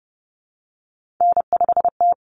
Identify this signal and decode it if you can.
Morse code, heard as a keyed tone, about 30 wpm, 700 Hz. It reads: N5T
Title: Common Cut Numbers - 3-Digit 30wpm